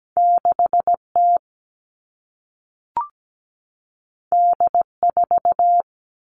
Morse code, heard as a keyed tone, about 17 wpm, 700 Hz. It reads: 6T D4
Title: All Cut Numbers - 2-Digit 17wpm